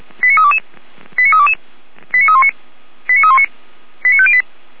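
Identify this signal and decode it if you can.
Five-tone selective call (ZVEI), heard as quick runs of five tones, tones of about 70 ms. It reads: 89322 89322 89319 89319 89508
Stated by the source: ZVEI five-tone call 89109 received on an SDR.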